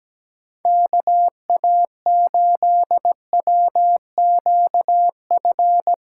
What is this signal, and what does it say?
Morse code, heard as a keyed tone, about 17 wpm, 700 Hz. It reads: KA8WQF
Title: USA Call Signs - 6 Characters - 8x Character Spacing - 17wpm